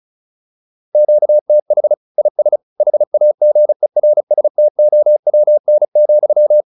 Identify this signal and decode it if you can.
Morse code, heard as a keyed tone, about 34 wpm, 600 Hz. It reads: QTH IS HAGERSTOWN,